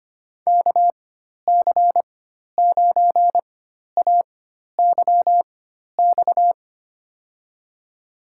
Morse code, heard as a keyed tone, about 25 wpm, spaced as for 12 wpm, 700 Hz. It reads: KC9AYX
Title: USA Call Signs - 6 Characters - 4x Character Spacing - 25wpm